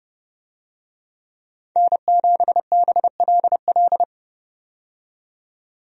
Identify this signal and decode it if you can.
Morse code, heard as a keyed tone, about 32 wpm, 700 Hz. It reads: N7BLL
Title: USA Call Signs - 5 Characters - 30wpm